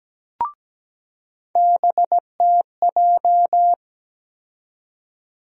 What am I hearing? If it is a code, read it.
Morse code, heard as a keyed tone, about 17 wpm, 700 Hz. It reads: BTJ
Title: USA Call Sign Postfixes - 3 Letters 17wpm